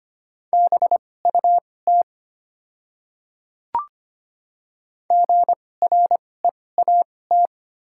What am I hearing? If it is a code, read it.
Morse code, heard as a keyed tone, about 25 wpm, spaced as for 18 wpm, 700 Hz. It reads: BUT GREAT
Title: 2x Character Spacing - Top 300 Words - 25wpm